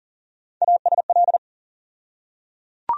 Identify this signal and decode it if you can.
Morse code, heard as a keyed tone, about 40 wpm, 700 Hz. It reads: ASL
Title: USA Call Sign Postfixes - 3 Letters 40wpm